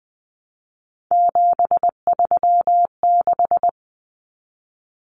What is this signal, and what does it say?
Morse code, heard as a keyed tone, about 20 wpm, 700 Hz. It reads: 736
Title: Common Cut Numbers - 3-Digit 20wpm